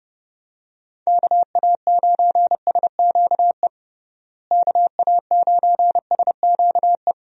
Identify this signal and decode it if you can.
Morse code, heard as a keyed tone, about 30 wpm, 700 Hz. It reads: KA9SQE KA9SQE
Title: USA Call Signs - 6 Characters - Repeated 2x - 30wpm